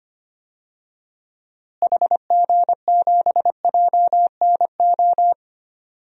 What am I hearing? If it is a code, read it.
Morse code, heard as a keyed tone, about 25 wpm, 700 Hz. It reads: HG7JNO
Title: DXCC-100 Call Signs - 6 Characters - Repeated 2x - 25wpm